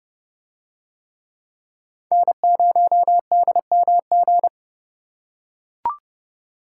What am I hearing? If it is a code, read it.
Morse code, heard as a keyed tone, about 29 wpm, 700 Hz. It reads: N0DMG